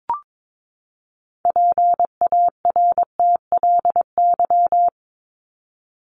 Morse code, heard as a keyed tone, about 22 wpm, 700 Hz. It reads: PARTLY
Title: LY Suffix - Top 100 Words with Base Words 22wpm